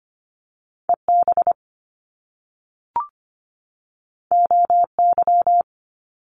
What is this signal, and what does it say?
Morse code, heard as a keyed tone, about 25 wpm, 700 Hz. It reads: EB OY
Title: Top Two Letter Combinations in Words 25wpm